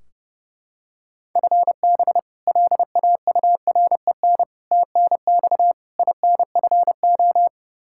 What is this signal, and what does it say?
Morse code, heard as a keyed tone, about 30 wpm, 700 Hz. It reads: FB LAUREN TNX INFO
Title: K1USN SST QSO Practice 30wpm